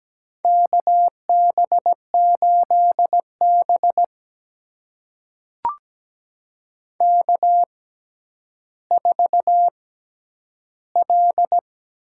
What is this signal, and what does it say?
Morse code, heard as a keyed tone, about 17 wpm, spaced as for 10 wpm, 700 Hz. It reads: KB8B K4L
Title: USA Call Signs - 4 Characters - 6x Character Spacing - 17wpm